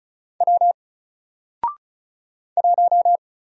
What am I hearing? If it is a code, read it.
Morse code, heard as a keyed tone, about 36 wpm, 700 Hz. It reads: W 1
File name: Single-Letter-Number-35wpm